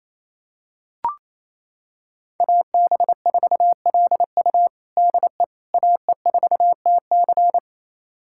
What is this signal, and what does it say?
Morse code, heard as a keyed tone, about 28 wpm, 700 Hz. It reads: AB4LU DE AE4TC
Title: USA Callsign DE Callsign - 5 Characters - 28wpm